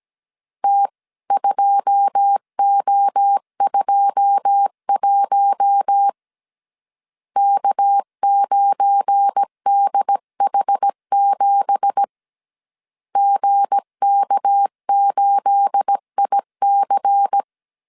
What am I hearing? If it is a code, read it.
Morse code, heard as a keyed tone, about 17 wpm, 800 Hz. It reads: T2O21 K9DH7 GK8IC
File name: CW / DET / 1 9 1500C S C 16wpm 2909